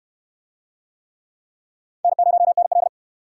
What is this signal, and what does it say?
Morse code, heard as a keyed tone, about 68 wpm, 700 Hz. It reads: N0NL